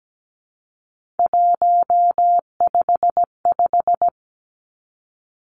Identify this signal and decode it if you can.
Morse code, heard as a keyed tone, about 17 wpm, 700 Hz. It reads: 155
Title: Three Challenging Letter-Number Combinations 17wpm